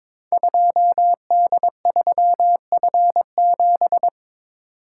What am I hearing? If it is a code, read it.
Morse code, heard as a keyed tone, about 22 wpm, 690 Hz. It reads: 2D3F7
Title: Code Groups - 5 Letter:Numbers - 22wpm